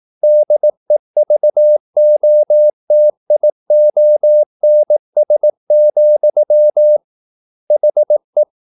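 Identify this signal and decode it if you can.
Morse code, heard as a keyed tone, about 18 wpm, 600 Hz. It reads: DEVOTIONS, HE
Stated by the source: Morse practice at 18 wpm, English text, from CW med Gnister.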